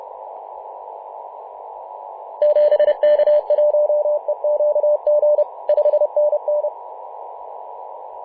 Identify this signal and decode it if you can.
Morse code, heard as a keyed tone, about 30 wpm, 600 Hz, with radio noise. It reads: 7K1EQG 5NN
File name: Mar/20 2349z FW5JJ 28.011MHz CW